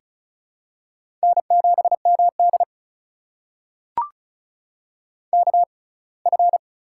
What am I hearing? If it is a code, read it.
Morse code, heard as a keyed tone, about 36 wpm, spaced as for 24 wpm, 700 Hz. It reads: N7MD KF
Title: USA Call Signs - 6x Character Spacing - 35wpm